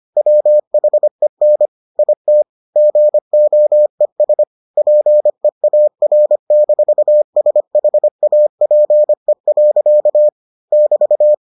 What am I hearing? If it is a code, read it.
Morse code, heard as a keyed tone, about 25 wpm, 600 Hz. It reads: WHEN IT GOES PEAR-SHAPE. <BT>